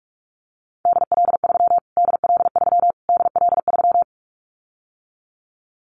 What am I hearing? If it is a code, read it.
Morse code, heard as a keyed tone, about 44 wpm, 700 Hz. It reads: DL3 DL3 DL3
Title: Callsign Prefixes for Top 25 DXCC - Repeated 3x - 45wpm